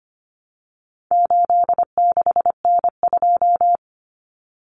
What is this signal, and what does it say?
Morse code, heard as a keyed tone, about 25 wpm, 700 Hz. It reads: 86N2
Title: Common Cut Numbers - 4-Digit 25wpm